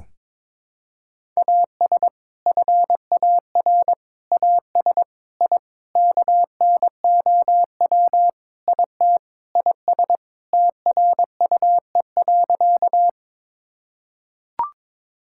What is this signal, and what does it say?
Morse code, heard as a keyed tone, about 22 wpm, 700 Hz. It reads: AS FAR AS I KNOW IT IS TRUE.
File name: practice / 22wpm / Sentences with 4-Word Phrases - 22wpm